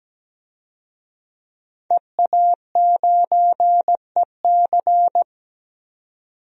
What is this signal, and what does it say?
Morse code, heard as a keyed tone, about 17 wpm, 700 Hz. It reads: EA9EC
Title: DXCC-100 Call Signs - 5 Characters - 17wpm